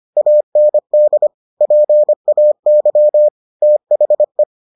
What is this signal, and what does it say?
Morse code, heard as a keyed tone, about 25 wpm, 600 Hz. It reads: AND PAY THE